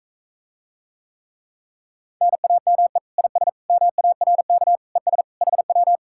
Morse code, heard as a keyed tone, about 45 wpm, 700 Hz. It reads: NAME IS MARK ES HW